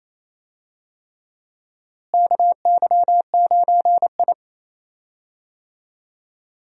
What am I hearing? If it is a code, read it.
Morse code, heard as a keyed tone, about 28 wpm, 700 Hz. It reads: KY9I
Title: USA Call Signs - 4 Characters - Repeated 3x - 28wpm